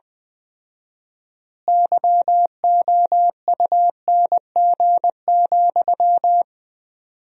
Morse code, heard as a keyed tone, about 20 wpm, 700 Hz. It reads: YOUNG,
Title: 4x Word-Spacing - Sentences from Top 400 Words - 20wpm